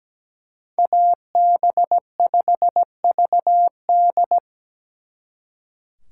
Morse code, heard as a keyed tone, about 20 wpm, 700 Hz. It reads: AB5VD
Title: Speed Racing - USA 2x2 Call Signs 17wpm